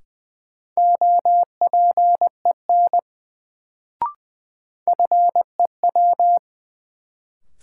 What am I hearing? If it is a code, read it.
Morse code, heard as a keyed tone, about 20 wpm, 700 Hz. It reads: OPEN FEW
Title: Sentences from Top 300 Words - New Word Drill - 20wpm